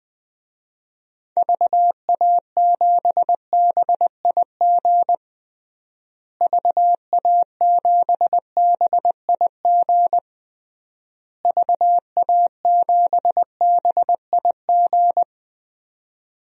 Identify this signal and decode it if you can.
Morse code, heard as a keyed tone, about 20 wpm, 700 Hz. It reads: VA7BIG VA7BIG VA7BIG
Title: DXCC-100 Call Signs - 6 Characters - Repeated 3x - 20wpm